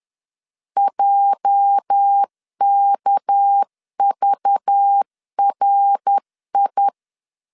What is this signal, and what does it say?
Morse code, heard as a keyed tone, about 10 wpm, 800 Hz. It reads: JKVRI
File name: LEZIONI / 9 286 10wpm 2735